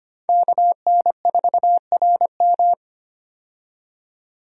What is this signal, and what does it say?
Morse code, heard as a keyed tone, about 25 wpm, 700 Hz. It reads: KN4RM
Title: USA Call Signs - 5 Characters - 25wpm